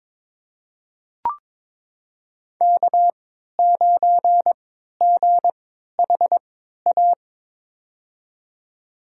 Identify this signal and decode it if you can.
Morse code, heard as a keyed tone, about 23 wpm, spaced as for 13 wpm, 700 Hz. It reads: K9GHA
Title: USA Call Signs - 5 Characters - 3x Character Spacing - 22wpm